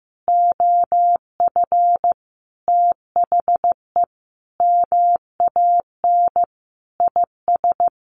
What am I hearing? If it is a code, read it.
Morse code, heard as a keyed tone, about 15 wpm, 700 Hz. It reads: OF THE MAN IS
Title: Sentences from Top 1400 Words 15wpm